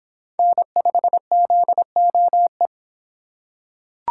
Morse code, heard as a keyed tone, about 25 wpm, 700 Hz. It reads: N5ZOE